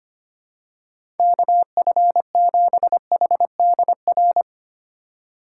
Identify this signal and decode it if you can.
Morse code, heard as a keyed tone, about 25 wpm, 700 Hz. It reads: KF7HDR
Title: USA Call Signs - 6 Characters - 3x Character Spacing - 25wpm